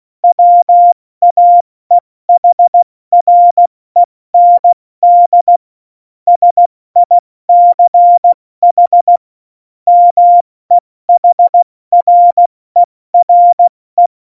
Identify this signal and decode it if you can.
Morse code, heard as a keyed tone, about 16 wpm, spaced as for 14 wpm, 700 Hz. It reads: WAEHREND SICH MEHRERE
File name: Morsekurs / Witze / Witz 22. Dezember 2020, 70 F80